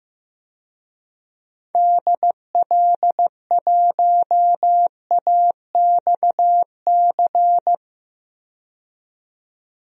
Morse code, heard as a keyed tone, about 15 wpm, 700 Hz. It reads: DL1AXC
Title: DXCC-100 Call Signs - 6 Characters - 15wpm